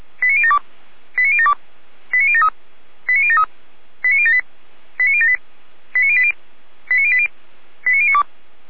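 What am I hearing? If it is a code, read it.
Five-tone selective call (ZVEI), heard as quick runs of five tones, tones of about 70 ms. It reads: 89072 89072 89073 89073 89078 89078 89080 89080 89092
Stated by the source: Five-tone ZVEI selective call received on an SDR.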